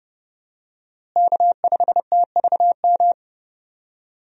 Morse code, heard as a keyed tone, about 32 wpm, 700 Hz. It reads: K5TVM
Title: USA Call Signs - 5 Characters - Repeated 2x - 30wpm